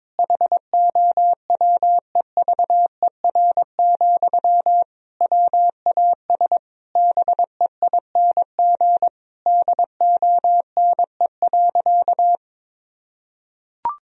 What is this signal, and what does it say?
Morse code, heard as a keyed tone, about 22 wpm, 700 Hz. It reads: HOWEVER, WAS BEING DONE.
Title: Sentences from Top 1000 Words - Review Part 3 - 22wpm